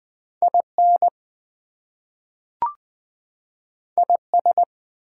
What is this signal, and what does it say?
Morse code, heard as a keyed tone, about 20 wpm, 700 Hz. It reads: IN IS
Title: Two Letter Words - 20wpm